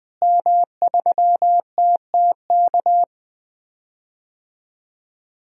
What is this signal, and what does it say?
Morse code, heard as a keyed tone, about 21 wpm, 700 Hz. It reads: M3TTK
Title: DXCC-100 Call Signs - 5 Characters - Repeated 2x - 20wpm